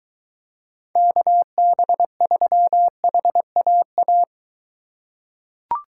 Morse code, heard as a keyed tone, about 22 wpm, 700 Hz. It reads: KB3HAA